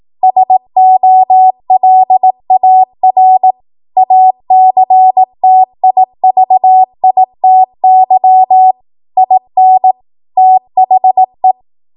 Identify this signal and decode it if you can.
Morse code, heard as a keyed tone, about 18 wpm, 750 Hz. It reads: SOLAR ACTIVITY IN THE
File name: Morse / 18WPM